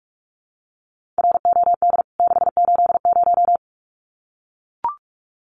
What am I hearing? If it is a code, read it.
Morse code, heard as a keyed tone, about 44 wpm, 700 Hz. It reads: ROD 680
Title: CWT Exchange 45wpm